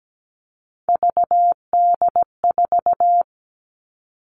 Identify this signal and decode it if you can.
Morse code, heard as a keyed tone, about 17 wpm, 700 Hz. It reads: VD4
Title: Three Challenging Letter-Number Combinations 17wpm